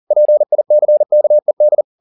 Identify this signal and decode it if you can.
Morse code, heard as a keyed tone, about 40 wpm, 600 Hz. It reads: PICKED
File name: the old man and the sea 2 40wpm